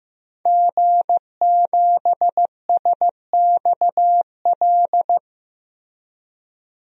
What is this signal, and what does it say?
Morse code, heard as a keyed tone, about 15 wpm, 700 Hz. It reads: G7SXL
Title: DXCC-100 Call Signs - 5 Characters - 15wpm